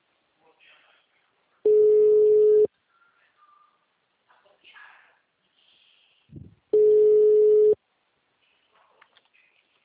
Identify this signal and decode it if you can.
Ringback tone (Europe, 425 Hz), with speech.